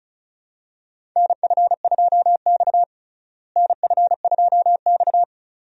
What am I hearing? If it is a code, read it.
Morse code, heard as a keyed tone, about 34 wpm, 700 Hz. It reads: NF2X NF2X
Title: USA Call Signs - 4 Characters - Repeated 2x - 35wpm